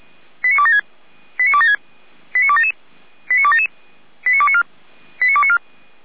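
Five-tone selective call (ZVEI), heard as quick runs of five tones, tones of about 70 ms. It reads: 89276 89276 89277 89277 89294 89294
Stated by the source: ZVEI five-tone selective call 89265, received via SDR.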